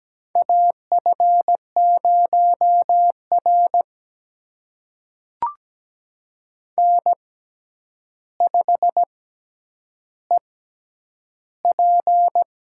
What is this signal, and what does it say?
Morse code, heard as a keyed tone, about 17 wpm, spaced as for 9 wpm, 700 Hz. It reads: AF0R N5EP
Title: USA Call Signs - 4 Characters - 6x Character Spacing - 17wpm